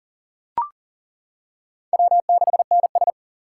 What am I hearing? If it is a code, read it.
Morse code, heard as a keyed tone, about 40 wpm, 700 Hz. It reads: W6NS
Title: K1USN SST QSO Practice - 40wpm